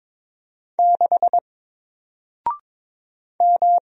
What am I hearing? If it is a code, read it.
Morse code, heard as a keyed tone, about 22 wpm, 700 Hz. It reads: 6 M
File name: Single-Letter-Number-22wpm